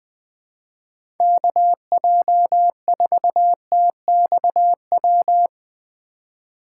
Morse code, heard as a keyed tone, about 20 wpm, 700 Hz. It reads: KJ4TXW
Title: USA Call Signs - 6 Characters - 3x Character Spacing - 20wpm